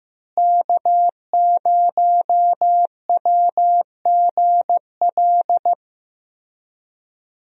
Morse code, heard as a keyed tone, about 15 wpm, 700 Hz. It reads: K0WGL
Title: USA Call Signs - 5 Characters - Repeated 3x - 15wpm